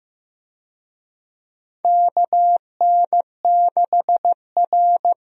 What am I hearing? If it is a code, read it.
Morse code, heard as a keyed tone, about 16 wpm, 700 Hz. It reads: KN6R
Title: USA Call Signs - 4 Characters - 15wpm